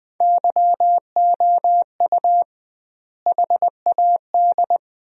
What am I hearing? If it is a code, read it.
Morse code, heard as a keyed tone, about 20 wpm, 700 Hz. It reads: YOU HAD
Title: 2x Word-Spacing - Sentences from Top 400 Words - 20wpm